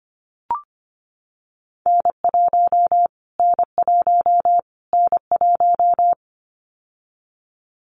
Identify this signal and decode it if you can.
Morse code, heard as a keyed tone, about 24 wpm, 700 Hz. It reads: N1 N1 N1
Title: Callsign Prefixes for Top 25 DXCC - Repeated 3x - 25wpm